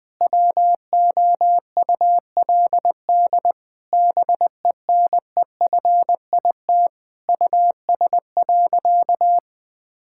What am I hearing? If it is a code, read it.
Morse code, heard as a keyed tone, about 20 wpm, 700 Hz. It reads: WOULD BENEFIT US.
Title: Sentences from Top 1000 Words - Review Part 6 - 20wpm